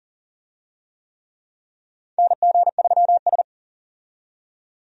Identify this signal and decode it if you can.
Morse code, heard as a keyed tone, about 40 wpm, 700 Hz. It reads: NG3S
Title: USA Call Signs - 4 Characters - 40wpm